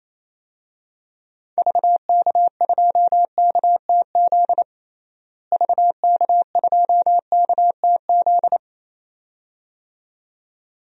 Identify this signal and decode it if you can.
Morse code, heard as a keyed tone, about 28 wpm, 700 Hz. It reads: VK2KTZ VK2KTZ
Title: DXCC-100 Call Signs - 6 Characters - Repeated 2x - 28wpm